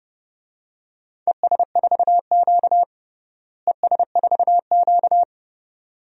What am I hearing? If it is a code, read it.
Morse code, heard as a keyed tone, about 32 wpm, 700 Hz. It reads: ES4Q ES4Q
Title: DXCC-100 Call Signs - 4 Characters - Repeated 2x - 30wpm